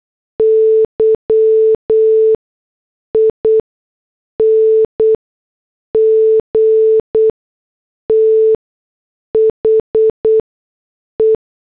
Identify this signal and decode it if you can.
Morse code, heard as a keyed tone, about 8 wpm, spaced as for 6 wpm, 440 Hz. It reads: YING THE